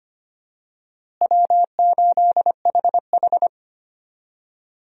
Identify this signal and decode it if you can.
Morse code, heard as a keyed tone, about 25 wpm, 700 Hz. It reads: W8HH
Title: USA Call Signs - 4 Characters - Repeated 3x - 25wpm